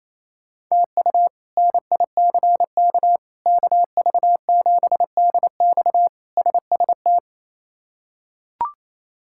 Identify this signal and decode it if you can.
Morse code, heard as a keyed tone, about 28 wpm, 700 Hz. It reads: TU NICK KV7DX SST
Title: K1USN SST QSO Practice - 28wpm